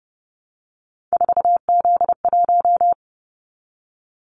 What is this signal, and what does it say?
Morse code, heard as a keyed tone, about 30 wpm, 700 Hz. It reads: 4Z1
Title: Callsign Prefixes for Top 50 DXCC - 30wpm